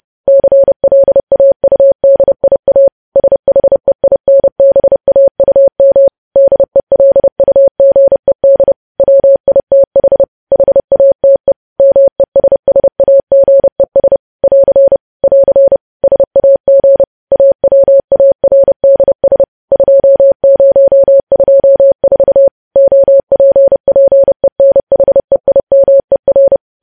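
Morse code, heard as a keyed tone, about 30 wpm, 550 Hz. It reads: CLAUDIA SHEINBAUM DELUGED WITH HATE MESSAGES <AR> <AR> SAG AWARDS 2024 OPPENHEIMER
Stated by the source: Morse news headlines sent at 30 wpm, opening with vvv.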